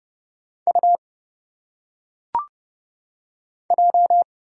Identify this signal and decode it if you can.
Morse code, heard as a keyed tone, about 30 wpm, 700 Hz. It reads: U J
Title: Single Letters - Phonetic Pronunciation 30wpm